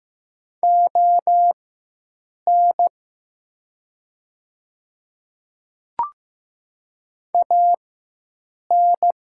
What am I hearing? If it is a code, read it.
Morse code, heard as a keyed tone, about 15 wpm, spaced as for 7 wpm, 700 Hz. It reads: ON AN
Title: 4x Character Spacing - Top 100 Words - 15wpm